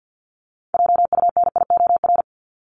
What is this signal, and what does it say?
Morse code, heard as a keyed tone, about 50 wpm, 700 Hz. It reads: JUNIOR
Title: Top 751-1000 Words 50wpm